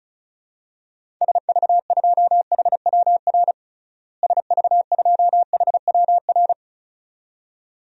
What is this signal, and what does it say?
Morse code, heard as a keyed tone, about 36 wpm, 700 Hz. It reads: SV2HWR SV2HWR
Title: DXCC-100 Call Signs - 6 Characters - Repeated 2x - 35wpm